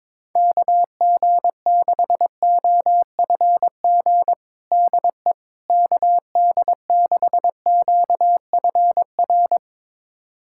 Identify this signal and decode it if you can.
Morse code, heard as a keyed tone, about 22 wpm, 700 Hz. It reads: KG6OFG DE KD6QFR
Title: USA Callsign DE Callsign 22wpm